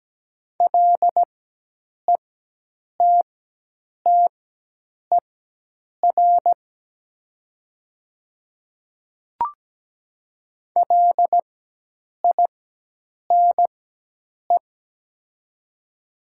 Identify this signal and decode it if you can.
Morse code, heard as a keyed tone, about 17 wpm, spaced as for 8 wpm, 700 Hz. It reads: LETTER LINE
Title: 4x Character Spacing - Top 500 Words - 17wpm